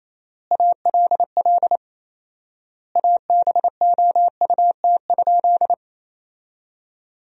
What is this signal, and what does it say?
Morse code, heard as a keyed tone, about 27 wpm, 700 Hz. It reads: ALL ABOUT?
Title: 4x Word-Spacing - Sentences from Top 100 Words - 28wpm